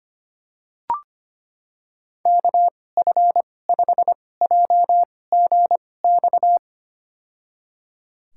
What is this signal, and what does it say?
Morse code, heard as a keyed tone, about 25 wpm, spaced as for 18 wpm, 700 Hz. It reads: KF5JGX
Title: USA Call Signs - 2x Character Spacing - 25wpm